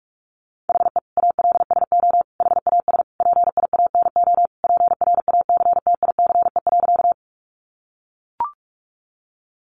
Morse code, heard as a keyed tone, about 44 wpm, 700 Hz. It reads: HE ALSO HAS PIANO PRACTICE.
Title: Additional Easy Phrases 45wpm